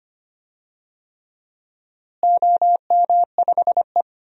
Morse code, heard as a keyed tone, about 25 wpm, 700 Hz. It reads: OM5E
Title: DXCC-100 Call Signs - 4 Characters - Repeated 3x - 25wpm